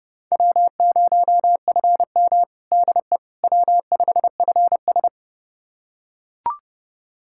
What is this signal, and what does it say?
Morse code, heard as a keyed tone, about 32 wpm, 700 Hz. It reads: W0FM DE W5FS
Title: USA Callsign DE Callsign - 4 Characters - 30wpm